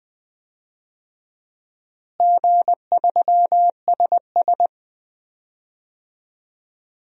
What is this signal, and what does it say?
Morse code, heard as a keyed tone, about 20 wpm, 700 Hz. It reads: G3SS
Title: DXCC-100 Call Signs - 4 Characters - Repeated 3x - 20wpm